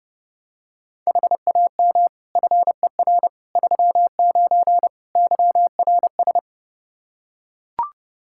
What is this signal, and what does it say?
Morse code, heard as a keyed tone, about 29 wpm, 700 Hz. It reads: HAM FER 39 YRS
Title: Practice QSO - Exchange 3 - More Info - 30wpm